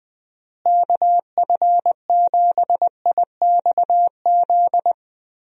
Morse code, heard as a keyed tone, about 21 wpm, 700 Hz. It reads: KF7IXZ
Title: USA Call Signs - 6 Characters - Repeated 2x - 20wpm